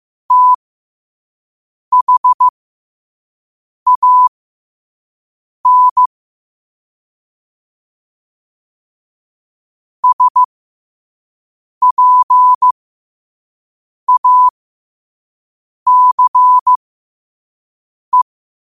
Morse code, heard as a keyed tone, about 15 wpm, spaced as for 5 wpm, 1000 Hz. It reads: THAN SPACE
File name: qotdfiles / 5wpm